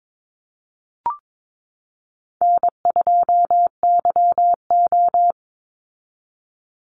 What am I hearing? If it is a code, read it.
Morse code, heard as a keyed tone, about 22 wpm, 700 Hz. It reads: N2YO
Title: Easy Call Signs Repeated 3x - 22wpm